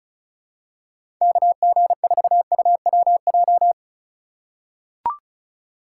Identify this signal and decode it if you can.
Morse code, heard as a keyed tone, about 36 wpm, 700 Hz. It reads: KG4UWJ
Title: USA Call Signs - 6 Characters - 4x Character Spacing - 35wpm